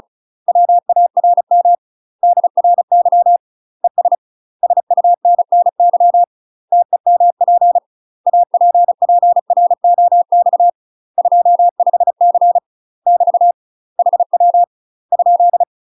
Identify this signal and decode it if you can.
Morse code, heard as a keyed tone, about 36 wpm, 700 Hz. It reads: WARM DRY ES SUNNY TEMP APPROX 25C <BT> HW ?
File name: UK MASTER QSOs TRAINING / 35WPM / UK MASTER QSOs 0033